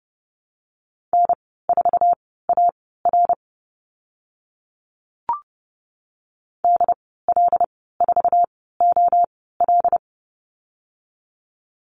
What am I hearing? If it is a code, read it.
Morse code, heard as a keyed tone, about 30 wpm, spaced as for 17 wpm, 700 Hz. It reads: N4AR DL4OL
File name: Lesson 14g - Number 2 - Call Signs 30wpm